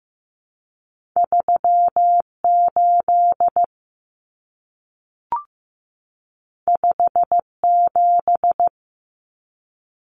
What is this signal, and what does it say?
Morse code, heard as a keyed tone, about 15 wpm, 700 Hz. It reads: 38 57
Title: Numbers - 2-Digit 15wpm